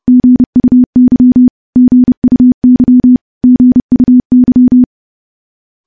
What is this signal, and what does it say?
Morse code, heard as a keyed tone, about 30 wpm, 250 Hz. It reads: GUY GUY GUY
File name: morse / morse-training-30WPM